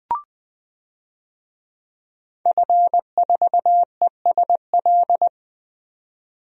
Morse code, heard as a keyed tone, about 20 wpm, 700 Hz. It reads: F4ESL
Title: DXCC-100 Call Signs - 5 Characters - 20wpm